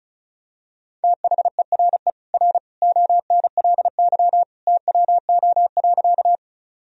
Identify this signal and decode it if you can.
Morse code, heard as a keyed tone, about 34 wpm, 690 Hz. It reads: THERE R ONLY TWO.
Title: Sentences from Top 200 Words and Abbreviations 35wpm